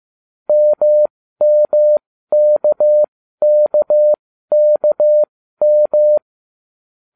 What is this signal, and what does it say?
Morse code, heard as a keyed tone, about 15 wpm, spaced as for 12 wpm, 600 Hz. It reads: MMKKKM